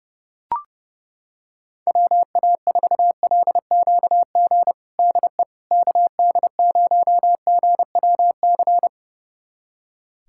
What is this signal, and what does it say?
Morse code, heard as a keyed tone, about 32 wpm, 700 Hz. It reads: WA4LQG DE KD0GWC
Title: USA Callsign DE Callsign - 6 Characters - 30wpm